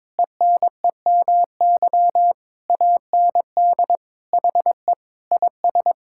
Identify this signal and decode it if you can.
Morse code, heard as a keyed tone, about 22 wpm, 700 Hz. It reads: ENEMY AND HE IS
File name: Sentences with 9-Word Phrases - 22wpm